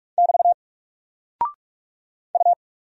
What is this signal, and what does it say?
Morse code, heard as a keyed tone, about 44 wpm, 700 Hz. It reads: <BT> U
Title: Single Character (Letter:Number:Punctuation:Prosign) 45wpm